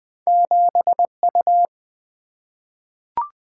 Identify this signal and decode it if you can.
Morse code, heard as a keyed tone, about 20 wpm, 700 Hz. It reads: 7U
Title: Single-Letter Single-Number 20wpm